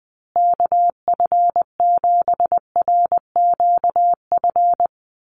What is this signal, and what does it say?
Morse code, heard as a keyed tone, about 20 wpm, 700 Hz. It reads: KF7RQF
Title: USA Call Signs Repeated 2x 20wpm